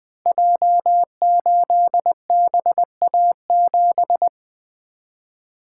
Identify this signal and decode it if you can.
Morse code, heard as a keyed tone, about 20 wpm, 700 Hz. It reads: J8BA7
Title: Code Groups - 5 Letter:Numbers - 20wpm